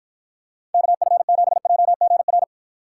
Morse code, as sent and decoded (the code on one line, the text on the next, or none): -.- ..-. --... .--- --. .-.
KF7JGR